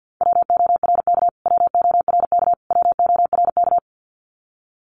Morse code, heard as a keyed tone, about 50 wpm, 700 Hz. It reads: WORK WORK WORK
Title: Top 500 Words Repeated 50wpm